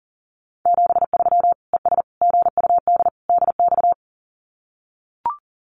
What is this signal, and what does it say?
Morse code, heard as a keyed tone, about 40 wpm, 700 Hz. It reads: 73 ES GUD DX
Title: Common QSO Phrases 40wpm